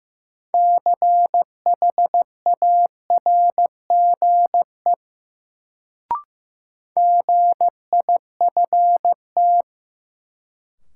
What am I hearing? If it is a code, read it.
Morse code, heard as a keyed tone, about 15 wpm, 700 Hz. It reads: CHARGE GIFT
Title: Sentences from Top 900 Words - New Word Drill - 15wpm